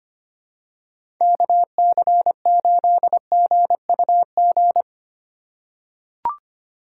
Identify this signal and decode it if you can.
Morse code, heard as a keyed tone, about 25 wpm, 690 Hz. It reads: KC8GUG
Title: USA Call Signs - 4x Character Spacing - 25wpm